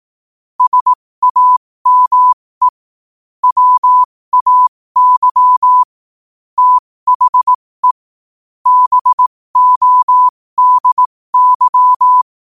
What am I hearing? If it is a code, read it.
Morse code, heard as a keyed tone, about 18 wpm, spaced as for 15 wpm, 1000 Hz. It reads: SAME WAY THE BODY
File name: qotdfiles / 15wpm